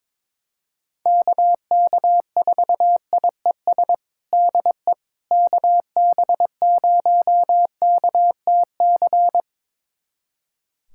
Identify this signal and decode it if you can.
Morse code, heard as a keyed tone, about 22 wpm, 700 Hz. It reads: KK4IES DE KB0KTC
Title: USA Callsign DE Callsign 22wpm